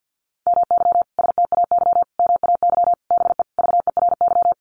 Morse code, heard as a keyed tone, about 50 wpm, 700 Hz. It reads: MY STAY MAY BE VERY
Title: Sentences from Top 1000 Words 50wpm